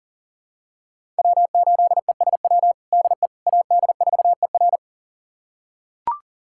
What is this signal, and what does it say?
Morse code, heard as a keyed tone, about 40 wpm, 700 Hz. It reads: W8ESW DE AD4ER
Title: USA Callsign DE Callsign - 5 Characters - 40wpm